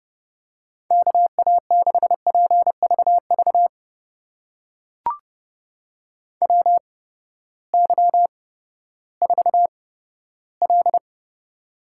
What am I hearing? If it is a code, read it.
Morse code, heard as a keyed tone, about 30 wpm, spaced as for 15 wpm, 700 Hz. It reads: KA6PVV WY4L
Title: USA Call Signs - 8x Character Spacing - 30wpm